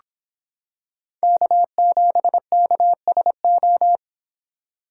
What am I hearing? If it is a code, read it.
Morse code, heard as a keyed tone, about 25 wpm, 700 Hz. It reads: K7KSO